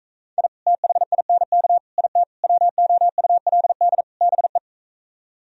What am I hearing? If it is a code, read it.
Morse code, heard as a keyed tone, about 45 wpm, 700 Hz. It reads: I THINK IT WOULD BE